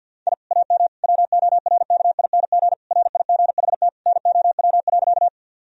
Morse code, heard as a keyed tone, about 50 wpm, 700 Hz. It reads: I AM WORKING RIGHT NOW.
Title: Sentences from Top 1000 Words - Review Part 4 - 50wpm